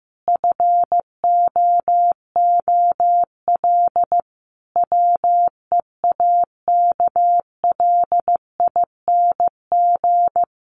Morse code, heard as a keyed tone, about 14 wpm, 700 Hz. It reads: FOOL WEAKLING
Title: A Sound Of Thunder 15wpm